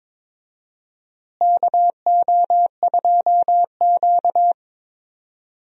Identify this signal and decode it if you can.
Morse code, heard as a keyed tone, about 22 wpm, 700 Hz. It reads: KO2Q
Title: USA Call Signs - 4 Characters - Repeated 2x - 22wpm